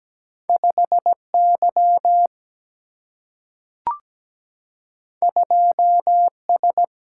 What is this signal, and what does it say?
Morse code, heard as a keyed tone, about 17 wpm, 700 Hz. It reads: 5 Y 2 S
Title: Single-Letter Single-Number 17wpm